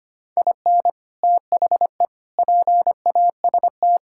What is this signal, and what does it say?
Morse code, heard as a keyed tone, about 25 wpm, 700 Hz. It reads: IN THE PAST
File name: practice / 25wpm / Sentences with 3-Word Phrases - 25wpm